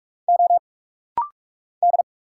Morse code, heard as a keyed tone, about 46 wpm, 700 Hz. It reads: O D
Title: Single Letter-Number Alt - Rapid-Fire 45wpm